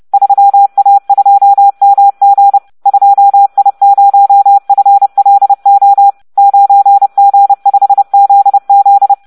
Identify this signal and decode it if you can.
Morse code, heard as a keyed tone, about 30 wpm, 800 Hz. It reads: 3A2MG 2I0FLO 9G5ZZ